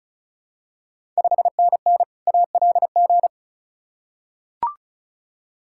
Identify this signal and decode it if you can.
Morse code, heard as a keyed tone, about 36 wpm, 700 Hz. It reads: 5NN ALG
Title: Ontario QSO Party Exchange 35wpm